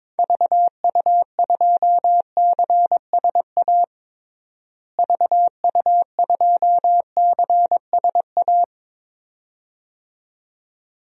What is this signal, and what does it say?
Morse code, heard as a keyed tone, about 21 wpm, 700 Hz. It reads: VU2CSA VU2CSA
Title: DXCC-100 Call Signs - 6 Characters - Repeated 2x - 22wpm